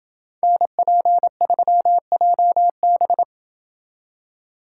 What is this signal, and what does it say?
Morse code, heard as a keyed tone, about 27 wpm, 700 Hz. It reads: NP3JB